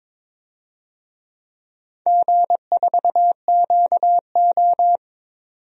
Morse code, heard as a keyed tone, about 22 wpm, 700 Hz. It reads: G4QO
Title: DXCC-100 Call Signs - 4 Characters - 22wpm